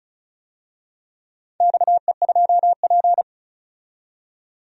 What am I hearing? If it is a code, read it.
Morse code, heard as a keyed tone, about 35 wpm, 700 Hz. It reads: XE2P
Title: DXCC-100 Call Signs - 4 Characters - 35wpm